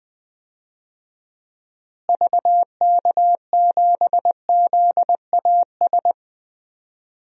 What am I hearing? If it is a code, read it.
Morse code, heard as a keyed tone, about 20 wpm, 700 Hz. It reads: VK7ZAS
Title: DXCC-100 Call Signs - 6 Characters - Repeated 2x - 20wpm